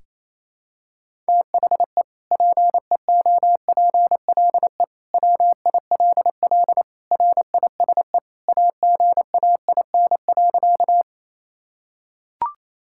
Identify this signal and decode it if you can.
Morse code, heard as a keyed tone, about 28 wpm, 700 Hz. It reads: THE PEOPLE WILL RISE AGAIN.
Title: Sentences from Top 1000 Words - Review Part 6 - 28wpm